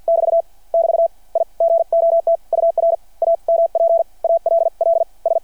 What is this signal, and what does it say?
Morse code, heard as a keyed tone, about 51 wpm, spaced as for 44 wpm, 660 Hz. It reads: <BT> <BT> I GOT UR AMP ALL S
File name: archived / CRSnet